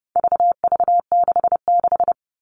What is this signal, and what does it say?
Morse code, heard as a keyed tone, about 30 wpm, 700 Hz. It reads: VV66
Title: All Cut Numbers - 4-Digit 30wpm